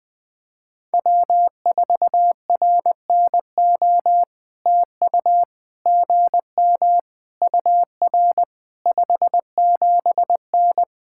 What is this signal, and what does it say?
Morse code, heard as a keyed tone, about 20 wpm, 700 Hz. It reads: W4RNO TU GM UR 57N
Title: POTA QSO Practice - 20wpm